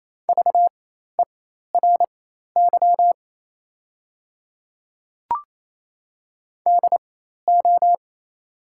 Morse code, heard as a keyed tone, about 28 wpm, spaced as for 13 wpm, 700 Hz. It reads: VERY DO
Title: 4x Character Spacing - Top 200 Words - 28wpm